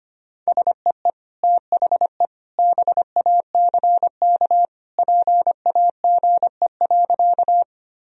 Morse code, heard as a keyed tone, about 25 wpm, 700 Hz. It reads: SEE THE BACK PAGE.
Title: Sentences from Top 1000 Words - Review Part 1 - 25wpm